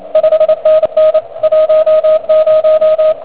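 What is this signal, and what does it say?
Morse code, heard as a keyed tone, about 28 wpm, 620 Hz, with radio noise. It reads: 5NN 10
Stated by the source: Quick exchange between two contest stations on 3559 kHz, just after a QRP contest.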